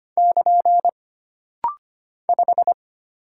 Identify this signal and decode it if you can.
Morse code, heard as a keyed tone, about 25 wpm, 700 Hz. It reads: <KN> 5
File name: Single Character (Letter:Number:Punctuation:Prosign) - Rapid-Fire - 25wpm